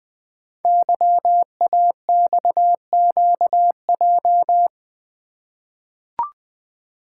Morse code, heard as a keyed tone, about 20 wpm, 700 Hz. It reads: YAXQJ
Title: Code Groups - 5 Letters - 20wpm